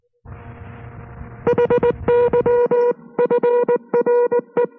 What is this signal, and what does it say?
Morse code, heard as a keyed tone, about 19 wpm, 480 Hz. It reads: HYFRE